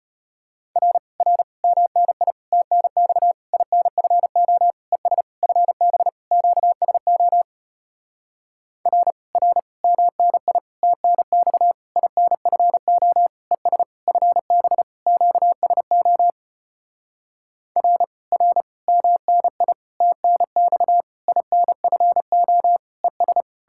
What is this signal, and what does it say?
Morse code, heard as a keyed tone, about 34 wpm, 700 Hz. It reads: R R MNI TNX INFO ES FB QSO R R MNI TNX INFO ES FB QSO R R MNI TNX INFO ES